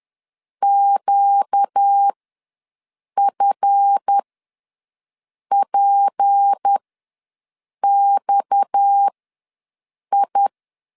Morse code, heard as a keyed tone, about 10 wpm, spaced as for 6 wpm, 800 Hz. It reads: QFPXI
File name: LEZIONI / 7 GF9Z / 7 GF ws 10wpm 3918